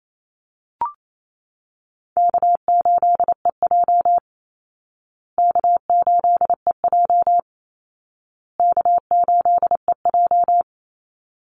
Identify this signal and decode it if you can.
Morse code, heard as a keyed tone, about 28 wpm, 700 Hz. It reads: K8EJ K8EJ K8EJ
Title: Easy Call Signs Repeated 3x - 28wpm